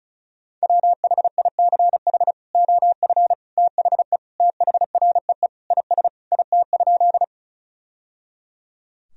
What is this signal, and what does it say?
Morse code, heard as a keyed tone, about 36 wpm, 700 Hz. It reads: WHICH OF THE THREE IS IT?
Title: Sentences from Top 1000 Words - Review Part 1 - 35wpm